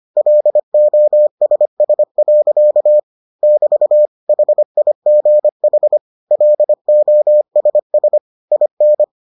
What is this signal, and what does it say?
Morse code, heard as a keyed tone, about 25 wpm, 600 Hz. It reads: LOSS. <BT> HIGH LOSS IN